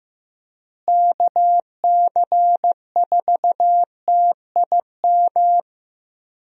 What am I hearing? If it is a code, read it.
Morse code, heard as a keyed tone, about 15 wpm, 700 Hz. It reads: KC4TIM
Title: USA Call Signs - 6 Characters - Repeated 3x - 15wpm